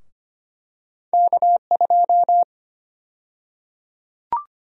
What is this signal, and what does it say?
Morse code, heard as a keyed tone, about 25 wpm, 700 Hz. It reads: K2
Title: USA Call Sign Prefixes - Repeated 2x - 25wpm